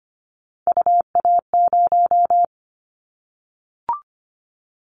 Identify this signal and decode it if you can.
Morse code, heard as a keyed tone, about 25 wpm, 700 Hz. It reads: UA0
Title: Callsign Prefixes for Top 50 DXCC - 25wpm